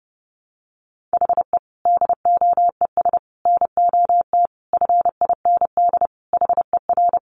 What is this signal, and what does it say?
Morse code, heard as a keyed tone, about 30 wpm, 700 Hz. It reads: HE DOES NOT FIND HER
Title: Easy Phrases 30wpm